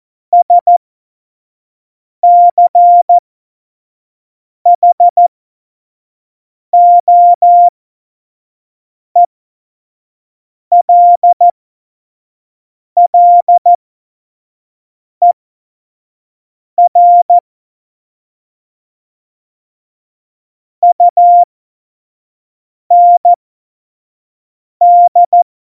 Morse code, heard as a keyed tone, about 14 wpm, spaced as for 5 wpm, 700 Hz. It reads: SCHOELLER UND
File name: Morsekurs / Witze / Witz 23. April 2019, 25 F70